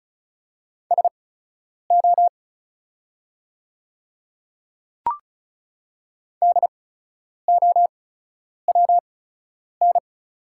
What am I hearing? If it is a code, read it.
Morse code, heard as a keyed tone, about 36 wpm, spaced as for 10 wpm, 700 Hz. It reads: SO DOWN
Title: 8x Character Spacing - Top 100 Words - 35wpm